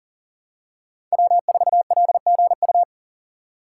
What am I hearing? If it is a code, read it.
Morse code, heard as a keyed tone, about 38 wpm, 700 Hz. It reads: W4LGU
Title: Code Groups - 5 Letter:Numbers - 40wpm